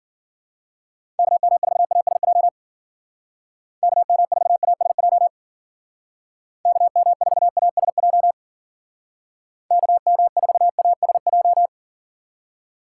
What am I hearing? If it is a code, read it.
Morse code, heard as a keyed tone, about 53 wpm, 700 Hz. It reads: KM4ASJ KM4ASJ KM4ASJ KM4ASJ